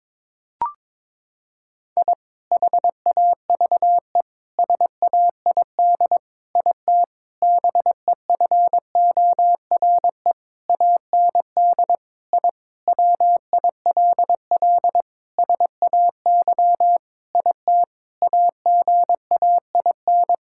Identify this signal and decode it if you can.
Morse code, heard as a keyed tone, about 22 wpm, 700 Hz. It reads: I HAVE SAID IT BEFORE AND I WILL SAY IT AGAIN
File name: practice / 22wpm / Sentences with 8-Word Phrases - 22wpm